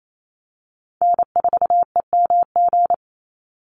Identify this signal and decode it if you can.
Morse code, heard as a keyed tone, about 28 wpm, 700 Hz. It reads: N4EMG
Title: NA SOTA Chaser Callsigns 28wpm